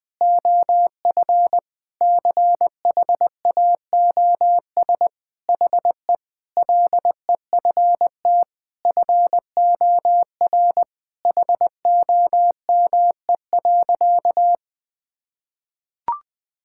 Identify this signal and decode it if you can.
Morse code, heard as a keyed tone, about 20 wpm, 700 Hz. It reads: OF CHAOS HE LEFT FOR HOME.
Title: Sentences with 4-Word Phrases - 20wpm